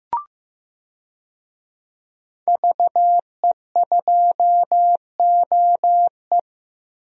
Morse code, heard as a keyed tone, about 15 wpm, 700 Hz. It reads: VE2OE
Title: DXCC-100 Call Signs - 5 Characters - Repeated 2x - 15wpm